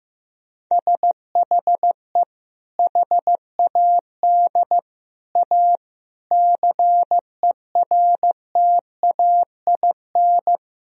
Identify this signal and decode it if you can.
Morse code, heard as a keyed tone, about 15 wpm, 700 Hz. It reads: SHE HAD A CERTAIN
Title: Sentences from Top 1000 Words - Review Part 4 - 15wpm